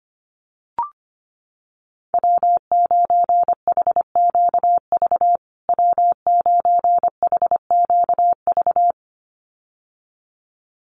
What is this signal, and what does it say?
Morse code, heard as a keyed tone, about 25 wpm, 700 Hz. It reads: W9HQV W9HQV
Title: USA Call Signs Repeated 2x 25wpm